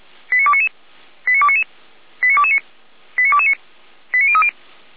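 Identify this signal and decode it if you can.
Five-tone selective call (ZVEI), heard as quick runs of five tones, tones of about 70 ms. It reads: 89299 89299 89229 89229 89030
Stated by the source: ZVEI five-tone selective call 89075 received with an SDR.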